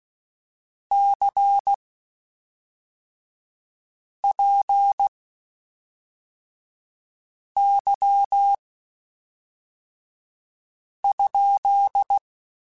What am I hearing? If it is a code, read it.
Morse code, heard as a keyed tone, about 16 wpm, spaced as for 3 wpm, 800 Hz. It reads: CPY?
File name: (spaced) / QSOs Spaced-mp3s / WA7QOX 1 @5 sp